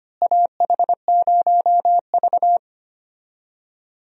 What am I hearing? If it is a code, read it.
Morse code, heard as a keyed tone, about 25 wpm, 700 Hz. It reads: AH0V
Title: USA Call Signs - 4 Characters - Repeated 3x - 25wpm